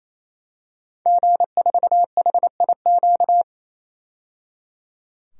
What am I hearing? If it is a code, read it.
Morse code, heard as a keyed tone, about 28 wpm, 700 Hz. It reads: G4HIQ
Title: DXCC-100 Call Signs - 5 Characters - 28wpm